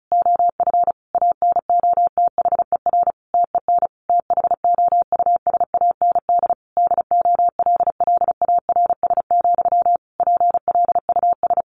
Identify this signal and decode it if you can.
Morse code, heard as a keyed tone, about 34 wpm, 700 Hz. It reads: OF ANOTHER TEN THOUSAND DOLLARS, PLUS
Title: A Sound Of Thunder 35wpm